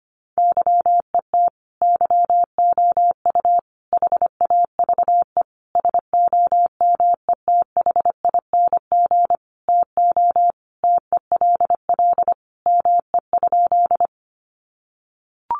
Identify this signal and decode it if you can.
Morse code, heard as a keyed tone, about 25 wpm, 700 Hz. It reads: YET YOU HAVE SOMETHING TO TELL ME?
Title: Sentences from Top 600 Words 25wpm